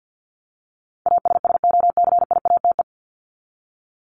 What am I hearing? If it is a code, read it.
Morse code, heard as a keyed tone, about 48 wpm, 700 Hz. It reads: ASSOCIATE
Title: Top 1751-2000 Words 50wpm